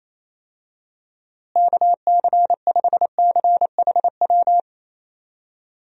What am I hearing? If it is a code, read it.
Morse code, heard as a keyed tone, about 28 wpm, 700 Hz. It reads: KC5CHW
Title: USA Call Signs - 6 Characters - Repeated 3x - 28wpm